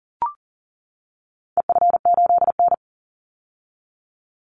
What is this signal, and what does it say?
Morse code, heard as a keyed tone, about 40 wpm, 700 Hz. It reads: EF8N
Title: Call Signs - 40wpm